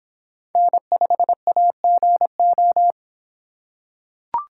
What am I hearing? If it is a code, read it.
Morse code, heard as a keyed tone, about 25 wpm, 700 Hz. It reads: N5AGO